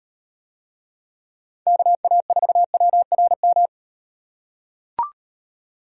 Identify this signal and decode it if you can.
Morse code, heard as a keyed tone, about 38 wpm, 700 Hz. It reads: KA4WRM